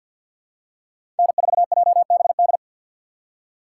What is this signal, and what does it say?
Morse code, heard as a keyed tone, about 48 wpm, 700 Hz. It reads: N4JBD
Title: USA Call Signs - 5 Characters - 4x Character Spacing - 50wpm